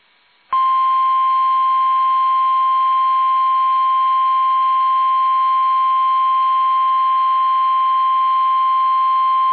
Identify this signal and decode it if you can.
Emergency Alert System (SAME) tones (the 1050 Hz attention tone). It is heard from 0.6 to 9.2 s.
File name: eas / KUSO / KUSO